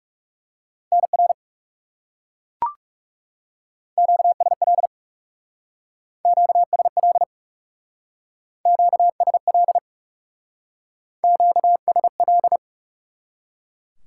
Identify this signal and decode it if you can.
Morse code, heard as a keyed tone, about 37 wpm, 700 Hz. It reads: NR QSL QSL QSL QSL
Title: Speed-Racing Top 90 QSO Elements 30wpm